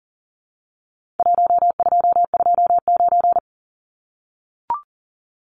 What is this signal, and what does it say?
Morse code, heard as a keyed tone, about 40 wpm, 700 Hz. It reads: 1229
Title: Numbers - Contest Style - 4-Digit 40wpm